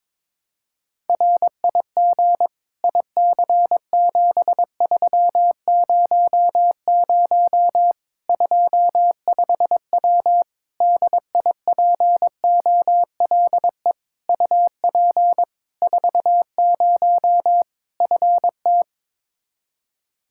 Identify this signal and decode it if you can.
Morse code, heard as a keyed tone, about 22 wpm, 700 Hz. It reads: RIG IC7300 25W DIPOLE UP 40 FT
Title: K1USN SST QSO Practice 22wpm